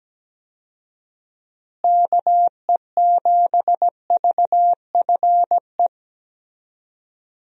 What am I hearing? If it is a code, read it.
Morse code, heard as a keyed tone, about 17 wpm, 700 Hz. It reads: KE7VFE
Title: USA Call Signs - 6 Characters - Repeated 3x - 17wpm